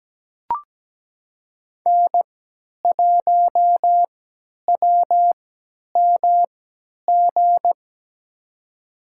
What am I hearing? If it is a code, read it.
Morse code, heard as a keyed tone, about 17 wpm, spaced as for 10 wpm, 700 Hz. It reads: N1WMG
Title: USA Call Signs - 3x Character Spacing - 17wpm